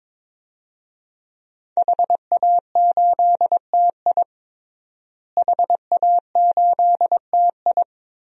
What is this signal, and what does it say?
Morse code, heard as a keyed tone, about 22 wpm, 700 Hz. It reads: HA8TI HA8TI
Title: DXCC-100 Call Signs - 5 Characters - Repeated 3x - 22wpm